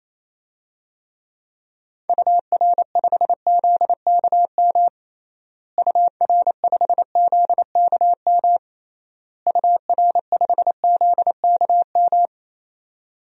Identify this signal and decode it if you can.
Morse code, heard as a keyed tone, about 28 wpm, 700 Hz. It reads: UR5ZKM UR5ZKM UR5ZKM
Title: DXCC-100 Call Signs - 6 Characters - Repeated 3x - 28wpm